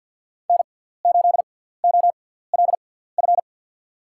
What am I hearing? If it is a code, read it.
Morse code, heard as a keyed tone, about 46 wpm, spaced as for 17 wpm, 700 Hz. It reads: N8OLF
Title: USA Call Signs - 5 Characters - 6x Character Spacing - 50wpm